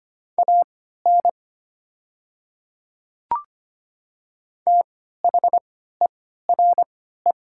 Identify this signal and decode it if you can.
Morse code, heard as a keyed tone, about 25 wpm, spaced as for 14 wpm, 700 Hz. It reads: AN THERE
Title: 3x Character Spacing - Top 100 Words - 25wpm